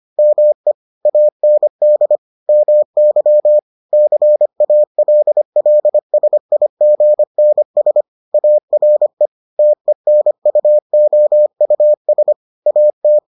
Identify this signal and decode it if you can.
Morse code, heard as a keyed tone, about 25 wpm, 600 Hz. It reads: ME AND MY CALLSIGNS ARE TENUOUS AT